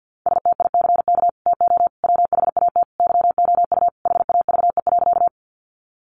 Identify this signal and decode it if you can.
Morse code, heard as a keyed tone, about 50 wpm, 700 Hz. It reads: STICK TO WHAT YOU HAVE.
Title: Sentences from Top 1200 Words 50wpm